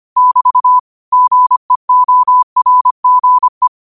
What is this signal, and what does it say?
Morse code, heard as a keyed tone, about 25 wpm, 1000 Hz. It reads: <BT> GEORGE
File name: qotdfiles / 25wpm